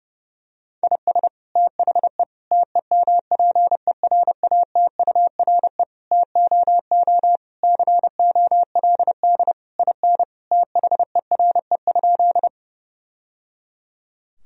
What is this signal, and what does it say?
Morse code, heard as a keyed tone, about 30 wpm, 700 Hz. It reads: IS THE TEMPERATURE TOO COLD IN THERE?
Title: Sentences from Top 1000 Words - Review Part 4 - 30wpm